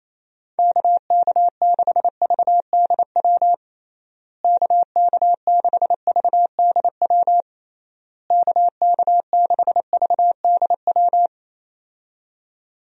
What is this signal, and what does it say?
Morse code, heard as a keyed tone, about 28 wpm, 700 Hz. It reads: KK6VDW KK6VDW KK6VDW
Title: USA Call Signs - 6 Characters - Repeated 3x - 28wpm